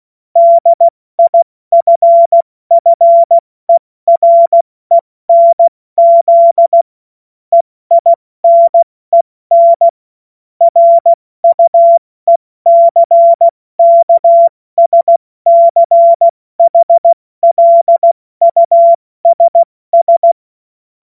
Morse code, heard as a keyed tone, about 16 wpm, spaced as for 14 wpm, 680 Hz. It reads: DIFFERENZ EINEN RUECKSCHLUSS